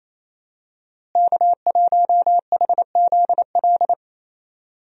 Morse code, heard as a keyed tone, about 28 wpm, 700 Hz. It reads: K1HZL
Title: USA Call Signs - 5 Characters - Repeated 3x - 28wpm